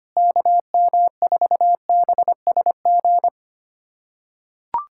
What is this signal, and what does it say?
Morse code, heard as a keyed tone, about 25 wpm, 700 Hz. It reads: KM4BSG
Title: USA Callsign DE Callsign - 6 Characters - 25wpm